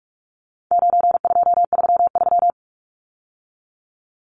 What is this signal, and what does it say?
Morse code, heard as a keyed tone, about 44 wpm, 700 Hz. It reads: 9233
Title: Numbers - 4-Digit 45wpm